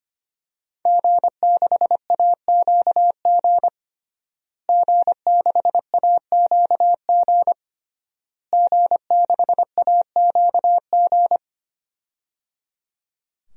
Morse code, heard as a keyed tone, about 25 wpm, 700 Hz. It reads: G6AQG G6AQG G6AQG
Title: DXCC-100 Call Signs - 5 Characters - Repeated 3x - 25wpm